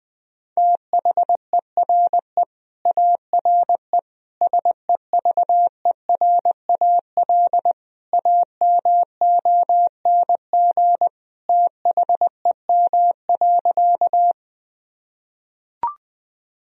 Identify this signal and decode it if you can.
Morse code, heard as a keyed tone, about 21 wpm, 700 Hz. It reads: THERE ARE SEVERAL AMONG THEM.
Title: Sentences from Top 1000 Words - Review Part 2 - 20wpm